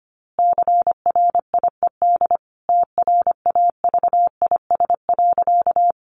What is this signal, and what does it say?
Morse code, heard as a keyed tone, about 25 wpm, 700 Hz. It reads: CRIED TRAVIS.
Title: A Sound Of Thunder 25wpm